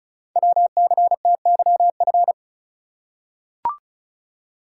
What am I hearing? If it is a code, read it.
Morse code, heard as a keyed tone, about 35 wpm, 690 Hz. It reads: WCTYF
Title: Code Groups - 5 Letter:Numbers - 35wpm